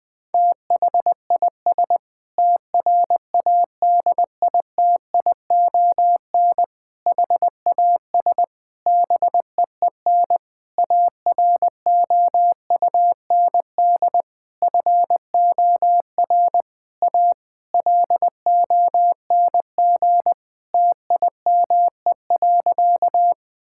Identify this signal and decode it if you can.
Morse code, heard as a keyed tone, about 19 wpm, 700 Hz. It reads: THIS TRADITION HAS BEEN AROUND FOR A LONG TIME.
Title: Sentences with 7-Word Phrases - Spoken Beginning - 20wpm